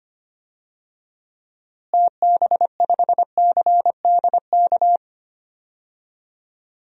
Morse code, heard as a keyed tone, about 25 wpm, 700 Hz. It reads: TB5CDK
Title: DXCC-100 Call Signs - 6 Characters - Repeated 2x - 25wpm